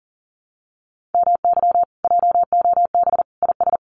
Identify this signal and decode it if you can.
Morse code, heard as a keyed tone, about 40 wpm, 700 Hz. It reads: MY JOB IS
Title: Common QSO Phrases Encore 40wpm